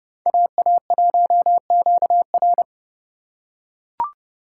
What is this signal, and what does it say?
Morse code, heard as a keyed tone, about 29 wpm, 700 Hz. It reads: AA1QR